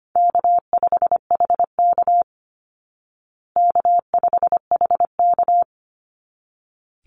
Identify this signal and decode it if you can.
Morse code, heard as a keyed tone, about 25 wpm, 700 Hz. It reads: K5HK K5HK
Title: Easy Call Signs Repeated 2x - 25wpm